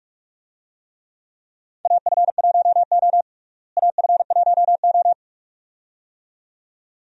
Morse code, heard as a keyed tone, about 42 wpm, 700 Hz. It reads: AF1O AF1O
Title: USA Call Signs - 4 Characters - Repeated 2x - 45wpm